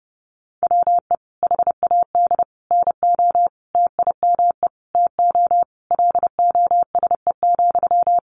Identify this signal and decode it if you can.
Morse code, heard as a keyed tone, about 30 wpm, 700 Hz. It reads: WE HAD NO TIME TO LOSE,
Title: Sentences from Top 1100 Words 30wpm